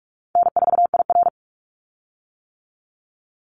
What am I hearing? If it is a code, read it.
Morse code, heard as a keyed tone, about 46 wpm, 700 Hz. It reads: N4IR
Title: Easy Call Signs - 45wpm